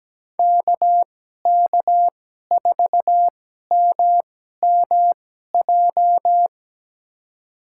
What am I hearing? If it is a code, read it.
Morse code, heard as a keyed tone, about 17 wpm, spaced as for 12 wpm, 700 Hz. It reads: KK4MMJ
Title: USA Call Signs - 6 Characters - 2x Character Spacing - 17wpm